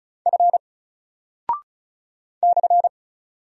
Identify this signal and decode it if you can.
Morse code, heard as a keyed tone, about 36 wpm, 700 Hz. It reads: F /
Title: Single Character (Letter:Number:Punctuation:Prosign) 35wpm